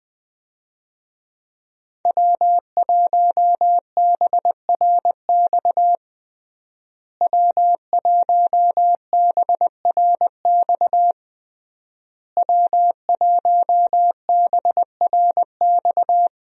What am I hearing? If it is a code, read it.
Morse code, heard as a keyed tone, about 20 wpm, 700 Hz. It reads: W1BRX W1BRX W1BRX
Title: USA Call Signs - 5 Characters - Repeated 3x - 20wpm